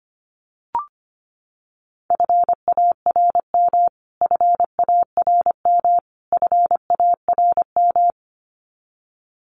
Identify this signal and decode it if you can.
Morse code, heard as a keyed tone, about 25 wpm, 700 Hz. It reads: FARM FARM FARM
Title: Top 500 Words Repeated 25wpm